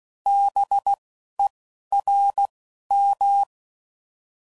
Morse code, heard as a keyed tone, about 16 wpm, 800 Hz. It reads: B E R M